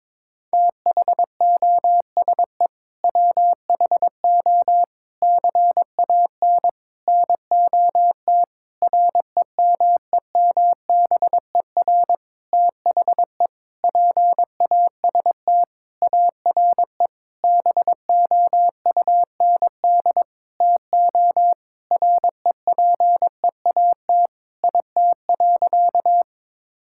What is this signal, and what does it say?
Morse code, heard as a keyed tone, about 22 wpm, 700 Hz. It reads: THOSE WHO CAN NOT REMEMBER THE PAST ARE BOUND TO REPEAT IT.
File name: practice / 22wpm / Sentences with 8-Word Phrases - 22wpm